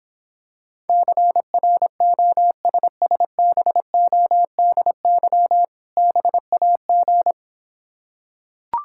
Morse code, heard as a keyed tone, about 26 wpm, 700 Hz. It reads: CROSSBODY BAG